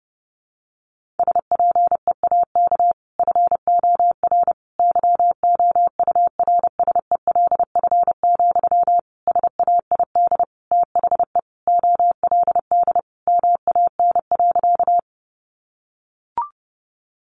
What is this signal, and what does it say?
Morse code, heard as a keyed tone, about 28 wpm, 700 Hz. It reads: SPEAK FOR YOURSELF, SAID THE OLD MAN.
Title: Sentences from Top 1100 Words 30wpm